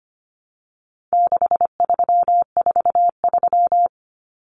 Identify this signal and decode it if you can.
Morse code, heard as a keyed tone, about 25 wpm, 700 Hz. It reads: 6343
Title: Common Cut Numbers - 4-Digit 25wpm